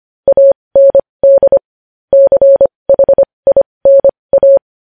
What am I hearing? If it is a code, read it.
Morse code, heard as a keyed tone, about 25 wpm, spaced as for 20 wpm, 550 Hz. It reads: AND CHINA
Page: news-headlines-in-morse-code-at-20-wpm